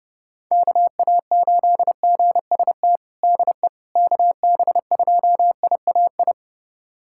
Morse code, heard as a keyed tone, about 30 wpm, 700 Hz. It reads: KA8GST DE KB2IAI
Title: USA Callsign DE Callsign - 6 Characters - 30wpm